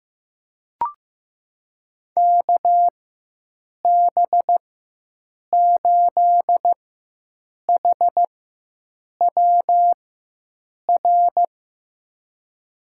Morse code, heard as a keyed tone, about 15 wpm, spaced as for 7 wpm, 700 Hz. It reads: KB8HWR
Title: USA Call Signs - 6 Characters - 4x Character Spacing - 15wpm